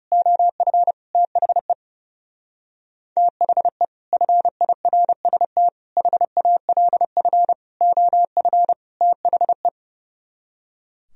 Morse code, heard as a keyed tone, about 37 wpm, 700 Hz. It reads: OF THE THE FIRST HALF OF THE
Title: Speed-Racing - Sets of 5 Words 30wpm